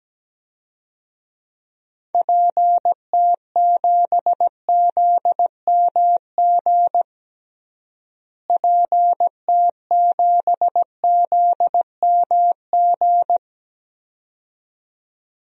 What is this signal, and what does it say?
Morse code, heard as a keyed tone, about 17 wpm, 700 Hz. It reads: PT7ZMG PT7ZMG
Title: DXCC-100 Call Signs - 6 Characters - Repeated 2x - 17wpm